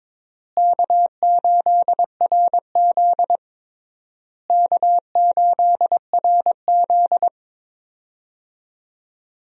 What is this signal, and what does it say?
Morse code, heard as a keyed tone, about 22 wpm, 700 Hz. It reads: K8RZ K8RZ
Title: USA Call Signs - 4 Characters - Repeated 3x - 22wpm